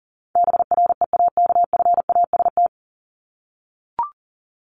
Morse code, heard as a keyed tone, about 40 wpm, 700 Hz. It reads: BREAKFAST
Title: Top 1251-1500 Words 40wpm